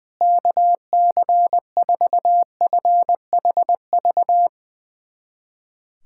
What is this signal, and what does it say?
Morse code, heard as a keyed tone, about 20 wpm, 700 Hz. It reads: KC4FHV
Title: USA Callsign DE Callsign - 6 Characters - 20wpm